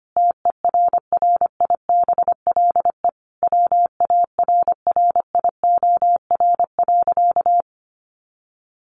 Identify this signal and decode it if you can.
Morse code, heard as a keyed tone, about 25 wpm, 700 Hz. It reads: TERRIBLE WARRIOR.
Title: A Sound Of Thunder 25wpm